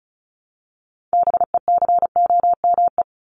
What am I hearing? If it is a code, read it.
Morse code, heard as a keyed tone, about 34 wpm, 700 Hz. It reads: BECOME
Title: Top 501-750 Words 35wpm